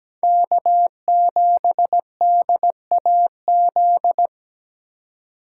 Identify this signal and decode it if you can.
Morse code, heard as a keyed tone, about 17 wpm, 700 Hz. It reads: K7DAZ
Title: USA Call Signs - 5 Characters - 3x Character Spacing - 17wpm